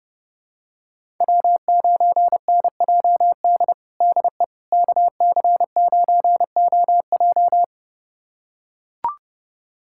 Morse code, heard as a keyed tone, about 30 wpm, 700 Hz. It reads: W9NJD DE KC9OJ
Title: USA Callsign DE Callsign - 5 Characters - 30wpm